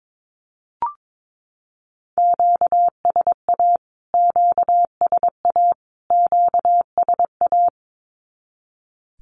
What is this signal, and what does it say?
Morse code, heard as a keyed tone, about 22 wpm, 700 Hz. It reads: QSA QSA QSA
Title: Q-Codes Repeated 3x 22wpm